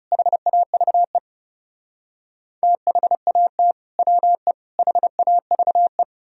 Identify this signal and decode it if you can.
Morse code, heard as a keyed tone, about 36 wpm, 700 Hz. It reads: HAVE THAT WE HAVE
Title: Speed-Racing - Sets of 3 Words 30wpm